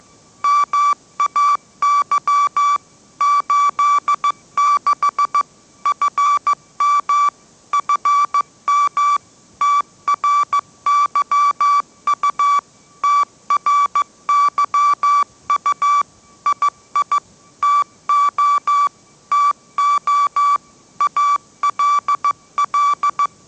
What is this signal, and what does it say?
Morse code, heard as a keyed tone, about 15 wpm, 1160 Hz. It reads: MAY 86 FM FM TRYU TRYU II TO TO ALL
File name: mrx / mrx066 16wpm